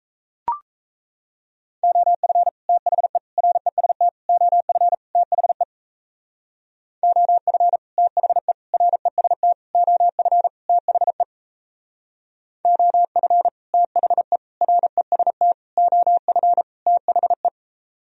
Morse code, heard as a keyed tone, about 37 wpm, 700 Hz. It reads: OF THE REST OF THE OF THE REST OF THE OF THE REST OF THE